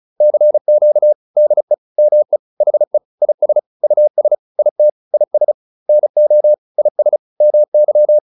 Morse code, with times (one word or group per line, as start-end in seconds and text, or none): 0.2-1.1 CQ
1.4-1.7 DE
2.0-2.4 ME
2.6-3.0 HE
3.2-3.6 IS
3.8-4.3 US
4.6-4.9 IT
5.1-5.5 IS
5.9-6.5 NO
6.8-7.2 IS
7.4-8.2 MY